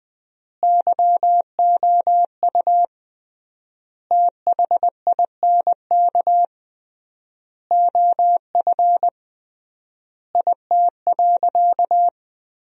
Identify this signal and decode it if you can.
Morse code, heard as a keyed tone, about 21 wpm, 700 Hz. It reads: YOU THINK OF IT.
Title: 3x Word-Spacing - Sentences from Top 200 Words - 20wpm